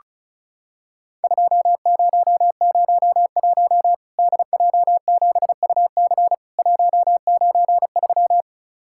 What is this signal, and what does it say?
Morse code, heard as a keyed tone, about 36 wpm, 700 Hz. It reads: 2001 DJ7UC 193
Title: Worked All Europe DX Contest - QTC Exchange - 35wpm